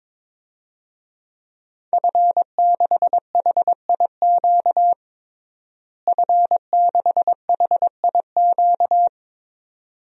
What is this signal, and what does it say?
Morse code, heard as a keyed tone, about 22 wpm, 700 Hz. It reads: F6HIQ F6HIQ
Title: DXCC-100 Call Signs - 5 Characters - Repeated 2x - 22wpm